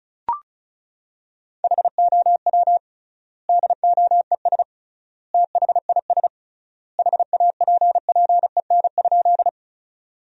Morse code, heard as a keyed tone, about 36 wpm, 700 Hz. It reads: HOW DOES THIS HAPPEN?
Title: 3x Word-Spacing - Sentences from Top 400 Words - 35wpm